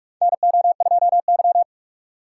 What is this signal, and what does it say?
Morse code, heard as a keyed tone, about 46 wpm, 700 Hz. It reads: NO2Y
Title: USA Call Signs - 2x Character Spacing - 45wpm